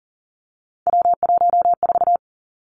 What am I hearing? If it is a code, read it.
Morse code, heard as a keyed tone, about 38 wpm, 700 Hz. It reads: W14
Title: Three Challenging Letter-Number Combinations 40wpm